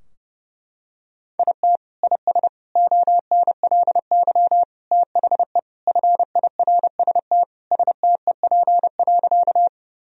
Morse code, heard as a keyed tone, about 30 wpm, 700 Hz. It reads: IT IS ONLY THE FIRST STEP.
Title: Sentences from Top 1000 Words - Review Part 2 - 30wpm